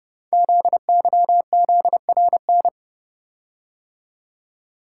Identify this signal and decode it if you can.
Morse code, heard as a keyed tone, about 30 wpm, 700 Hz. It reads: ZYZRN
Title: Code Groups - 5 Letters - 30wpm